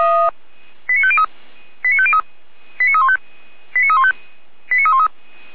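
Five-tone selective call (ZVEI), heard as quick runs of five tones, tones of about 70 ms. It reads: 89503 89503 89315 89315 89313
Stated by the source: ZVEI five-tone selective call received by SDR, logged by Feuerwehr Orlamünde.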